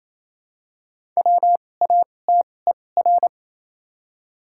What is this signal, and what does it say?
Morse code, heard as a keyed tone, about 28 wpm, spaced as for 20 wpm, 700 Hz. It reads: WATER
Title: 2x Character Spacing - Top 100 Words - 28wpm